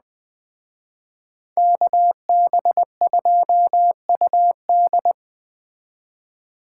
Morse code, heard as a keyed tone, about 21 wpm, 700 Hz. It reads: KB2UD